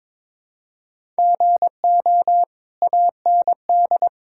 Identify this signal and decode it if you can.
Morse code, heard as a keyed tone, about 22 wpm, 700 Hz. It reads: GO AND
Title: Sentences from Top 1000 Words - Review Part 5 - 22wpm